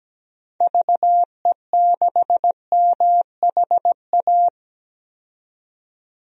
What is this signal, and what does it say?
Morse code, heard as a keyed tone, about 17 wpm, 700 Hz. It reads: VE6MHA
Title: DXCC-100 Call Signs - 6 Characters - 17wpm